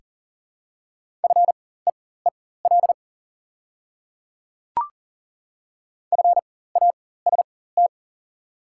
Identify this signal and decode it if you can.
Morse code, heard as a keyed tone, about 38 wpm, spaced as for 18 wpm, 690 Hz. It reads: FEEL FAST
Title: 4x Character Spacing - Top 500 Words - 40wpm